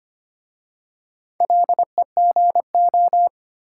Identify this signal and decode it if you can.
Morse code, heard as a keyed tone, about 25 wpm, 700 Hz. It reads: LEGO
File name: practice / 25wpm / Difficult Words with Hints - 25wpm